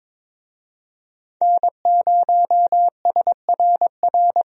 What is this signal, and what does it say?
Morse code, heard as a keyed tone, about 22 wpm, 700 Hz. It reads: N0SRR
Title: USA Call Signs - 5 Characters - Repeated 2x - 22wpm